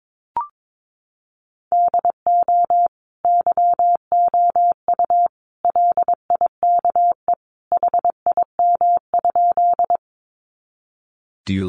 Morse code, heard as a keyed tone, about 22 wpm, 700 Hz. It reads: DO YOU LIKE HIM?
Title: Sentences from Top 100 Words 22wpm